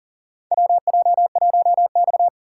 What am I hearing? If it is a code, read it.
Morse code, heard as a keyed tone, about 40 wpm, 700 Hz. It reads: WJ1X
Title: USA Call Signs - 4 Characters - 2x Character Spacing - 40wpm